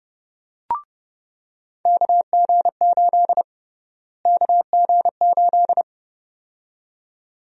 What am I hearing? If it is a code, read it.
Morse code, heard as a keyed tone, about 29 wpm, 700 Hz. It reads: KG8 KG8
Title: USA Call Sign Prefixes - Repeated 2x - 30wpm